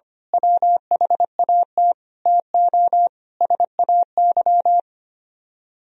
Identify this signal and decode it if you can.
Morse code, heard as a keyed tone, about 24 wpm, 700 Hz. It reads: WHAT TO SAY
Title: Speed-Racing - Sets of 5 Words 25wpm